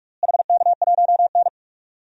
Morse code, heard as a keyed tone, about 46 wpm, 700 Hz. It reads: HK1N
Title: DXCC-100 Call Signs - 4 Characters - Repeated 2x - 45wpm